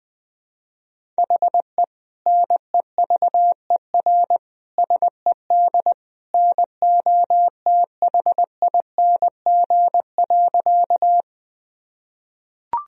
Morse code, heard as a keyed tone, about 20 wpm, 700 Hz. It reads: HE NEVER SED NOTHING.
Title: Sentences from Top 500 Words and Abbreviations 20wpm